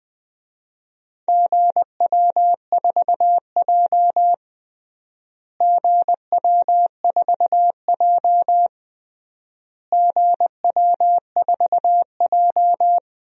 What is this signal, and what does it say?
Morse code, heard as a keyed tone, about 20 wpm, 700 Hz. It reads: GW4J GW4J GW4J
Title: DXCC-100 Call Signs - 4 Characters - Repeated 3x - 20wpm